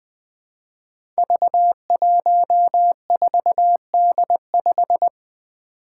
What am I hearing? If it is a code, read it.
Morse code, heard as a keyed tone, about 20 wpm, 700 Hz. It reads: V14D5
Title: Five Challenging Letter-Number Combinations 20wpm